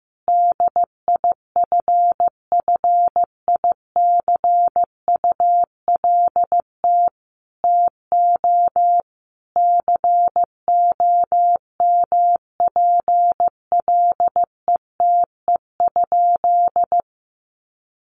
Morse code, heard as a keyed tone, about 15 wpm, 700 Hz. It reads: DIFFICULT TO COMPLETE?
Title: Sentences from Top 1500 Words 15wpm